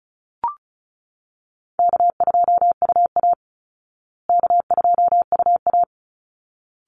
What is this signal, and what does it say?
Morse code, heard as a keyed tone, about 34 wpm, 700 Hz. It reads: K2UA K2UA
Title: Easy Call Signs Repeated 2x - 35wpm